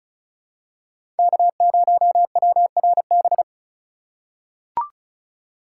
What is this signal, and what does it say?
Morse code, heard as a keyed tone, about 34 wpm, 700 Hz. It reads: K0WRB
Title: USA Call Signs - 2x Character Spacing - 35wpm